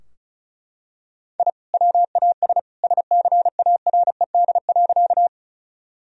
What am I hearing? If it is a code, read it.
Morse code, heard as a keyed tone, about 36 wpm, 700 Hz. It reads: I WAS SCARED.
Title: Sentences from Top 1000 Words - Review Part 4 - 35wpm